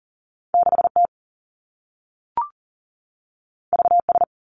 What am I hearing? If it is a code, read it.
Morse code, heard as a keyed tone, about 41 wpm, 700 Hz. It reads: 6 T V S
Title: Two Challenging Letter-Number Combinations 40wpm